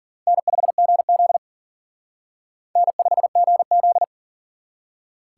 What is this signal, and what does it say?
Morse code, heard as a keyed tone, about 52 wpm, 700 Hz. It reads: N5GZ N5GZ